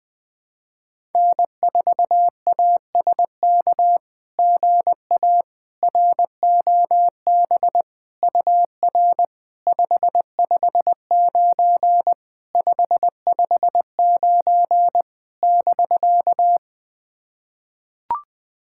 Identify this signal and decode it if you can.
Morse code, heard as a keyed tone, about 20 wpm, 690 Hz. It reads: N4ASK GA ROB UR 559 559 <BK>
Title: SOTA QSO Practice 20wpm